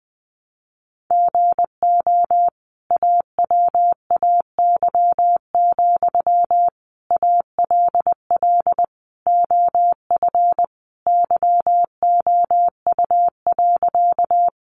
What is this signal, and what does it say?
Morse code, heard as a keyed tone, about 19 wpm, 700 Hz. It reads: GO AWAY, ALL OF YOU.
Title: Sentences from Top 600 Words 20wpm